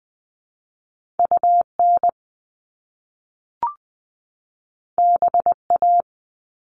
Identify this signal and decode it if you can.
Morse code, heard as a keyed tone, about 20 wpm, 700 Hz. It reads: U N B A
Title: Top Two Letter Combinations in Words 20wpm